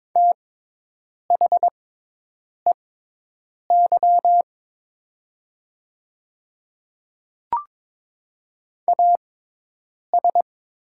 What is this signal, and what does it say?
Morse code, heard as a keyed tone, about 22 wpm, spaced as for 8 wpm, 700 Hz. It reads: THEY AS
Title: 6x Character Spacing - Top 100 Words - 22wpm